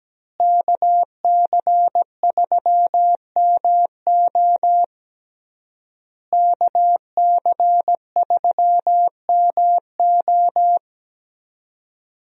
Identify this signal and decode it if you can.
Morse code, heard as a keyed tone, about 17 wpm, 700 Hz. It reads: KC3MO KC3MO
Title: USA Call Signs - 5 Characters - Repeated 3x - 17wpm